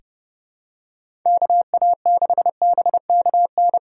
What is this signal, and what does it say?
Morse code, heard as a keyed tone, about 30 wpm, 700 Hz. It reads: KA6BKN
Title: USA Call Signs - 6 Characters - 3x Character Spacing - 30wpm